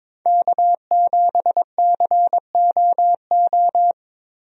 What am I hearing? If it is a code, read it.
Morse code, heard as a keyed tone, about 22 wpm, 700 Hz. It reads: K7COO
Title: USA Call Signs - 5 Characters - Repeated 3x - 22wpm